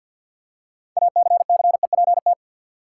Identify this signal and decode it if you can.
Morse code, heard as a keyed tone, about 53 wpm, 700 Hz. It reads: ACCEPT
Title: Six Letter Words - 50wpm